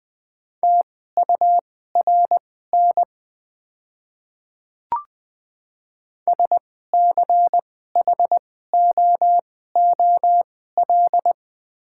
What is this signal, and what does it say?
Morse code, heard as a keyed tone, about 20 wpm, spaced as for 14 wpm, 700 Hz. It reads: TURN SCHOOL
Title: 2x Character Spacing - Top 300 Words - 20wpm